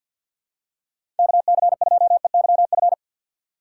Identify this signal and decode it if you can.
Morse code, heard as a keyed tone, about 50 wpm, 700 Hz. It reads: KC2EYF
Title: USA Call Signs - 6 Characters - 6x Character Spacing - 50wpm